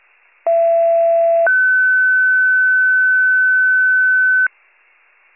Two-tone page, heard as a one-second tone, then a three-second tone, about 670 then 1500 Hz. It is heard from 0.5 to 4.5 s.